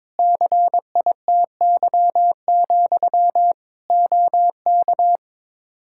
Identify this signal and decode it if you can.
Morse code, heard as a keyed tone, about 22 wpm, 700 Hz. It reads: CITY, OK
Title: Practice QSO - Exchange 1 - Beginning - 22wpm